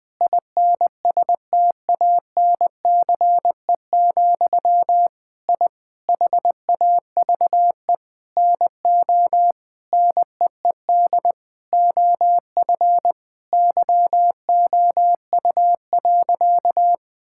Morse code, heard as a keyed tone, about 21 wpm, 700 Hz. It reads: INSTANCE, I HAVE NO NEED OF YOU.
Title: Sentences from Top 1000 Words - Review Part 5 - 20wpm